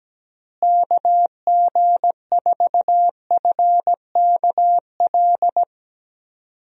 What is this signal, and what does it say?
Morse code, heard as a keyed tone, about 17 wpm, 700 Hz. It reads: KG4FKL
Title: USA Call Signs - 6 Characters - Repeated 2x - 17wpm